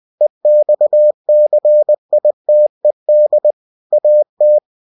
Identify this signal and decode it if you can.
Morse code, heard as a keyed tone, about 20 wpm, 600 Hz. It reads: EXCITED AT
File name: cw-books / winnie-the-pooh / winnie-20wpm-page-0062